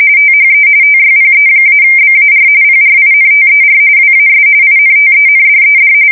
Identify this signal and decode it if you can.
RTTY, about 45.45 baud. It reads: THE TYPE-BAR PRINTER WAS INTENDED
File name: fsk170 / teletype-corp